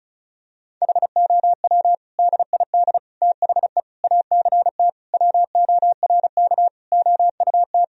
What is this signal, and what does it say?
Morse code, heard as a keyed tone, about 36 wpm, 700 Hz. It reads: HOW DID THE ACT WORK OUT
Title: Sentences from Top 1000 Words - Review Part 4 - 35wpm